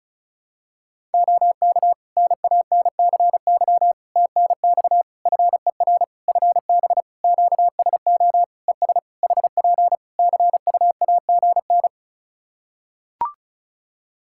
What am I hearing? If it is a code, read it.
Morse code, heard as a keyed tone, about 36 wpm, 700 Hz. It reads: OK NANCY TNX FER FB QSO ES HP CUAGN
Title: Practice QSO - Exchange 4 - Ending - 35wpm